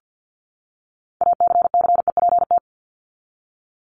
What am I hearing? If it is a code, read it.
Morse code, heard as a keyed tone, about 50 wpm, 700 Hz. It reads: ACCEPT
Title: Top 1001-1250 Words 50wpm